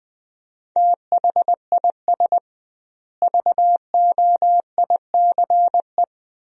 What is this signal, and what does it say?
Morse code, heard as a keyed tone, about 20 wpm, 700 Hz. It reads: THIS VOICE
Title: 2x Word-Spacing - Sentences from Top 500 Words - 20wpm